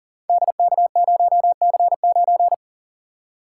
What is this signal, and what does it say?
Morse code, heard as a keyed tone, about 40 wpm, 700 Hz. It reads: DK0C9
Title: Code Groups - 5 Letter:Numbers - 40wpm